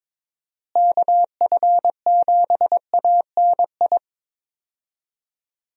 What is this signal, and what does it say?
Morse code, heard as a keyed tone, about 22 wpm, 700 Hz. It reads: KF7ANI
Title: USA Call Signs - 6 Characters - Repeated 2x - 22wpm